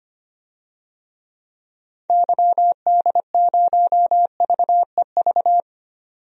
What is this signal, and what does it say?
Morse code, heard as a keyed tone, about 25 wpm, 700 Hz. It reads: YD0VEV
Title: DXCC-100 Call Signs - 6 Characters - 25wpm